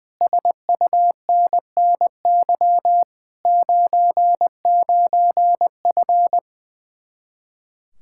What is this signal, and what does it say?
Morse code, heard as a keyed tone, about 20 wpm, 700 Hz. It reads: SUNNY 99F
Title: Speed-Racing Common QSO Phrases Encore - 20wpm